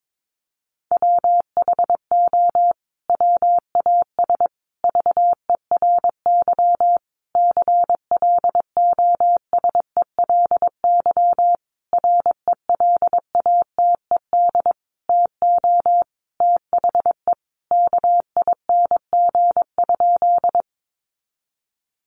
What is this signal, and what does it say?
Morse code, heard as a keyed tone, about 22 wpm, 700 Hz. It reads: WHO WAS VERY CLOSELY RELATED TO THE KING?
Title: Sentences from Top 1400 Words 22wpm